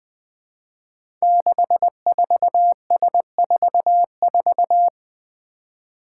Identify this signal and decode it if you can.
Morse code, heard as a keyed tone, about 20 wpm, 700 Hz. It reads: 64S44
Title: Five Challenging Letter-Number Combinations 20wpm